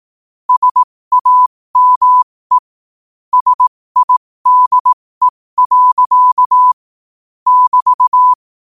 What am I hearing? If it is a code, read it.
Morse code, heard as a keyed tone, about 18 wpm, spaced as for 15 wpm, 1000 Hz. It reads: SAME SIDE. <BT>
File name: qotdfiles / 15wpm